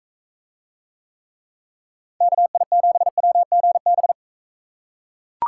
Morse code, heard as a keyed tone, about 40 wpm, 700 Hz. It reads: KI7WGB